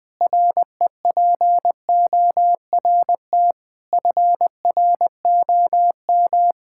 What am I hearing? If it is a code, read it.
Morse code, heard as a keyed tone, about 20 wpm, 700 Hz. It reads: REPORT FROM
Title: Sentences with 7-Word Phrases - 20wpm